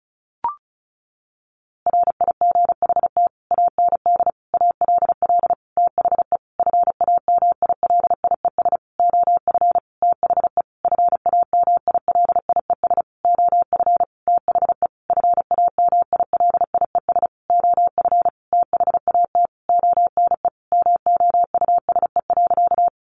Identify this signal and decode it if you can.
Morse code, heard as a keyed tone, about 34 wpm, 700 Hz. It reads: RIGHT AND ALL THE FAMILIES OF THE FAMILIES OF THE FAMILIES OF THAT ONE MOUSE.
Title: A Sound Of Thunder 35wpm